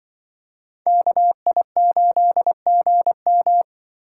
Morse code, heard as a keyed tone, about 24 wpm, 700 Hz. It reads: KI8GM
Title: Speed Racing - USA Call Signs 20wpm